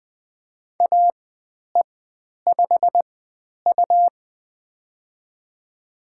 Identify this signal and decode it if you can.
Morse code, heard as a keyed tone, about 20 wpm, spaced as for 10 wpm, 700 Hz. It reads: AE5U
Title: 102 Lesson 17g - Letter u - Call Signs 20wpm